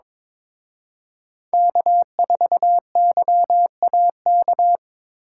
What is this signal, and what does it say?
Morse code, heard as a keyed tone, about 26 wpm, 700 Hz. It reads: K4YAK
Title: Speed Racing - USA 1x3 Call Signs 22wpm